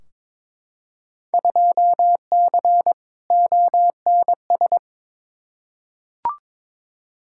Morse code, heard as a keyed tone, about 22 wpm, 700 Hz. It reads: 2C ONS
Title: Field Day Exchanges Encore - 22wpm